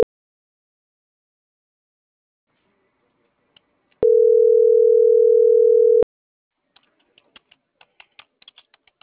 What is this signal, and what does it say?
Ringback tone (North America), 440 + 480 Hz.